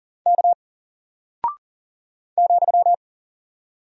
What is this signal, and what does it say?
Morse code, heard as a keyed tone, about 40 wpm, 700 Hz. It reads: K ,
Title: Single Character (Letter:Number:Punctuation:Prosign) Alt 40wpm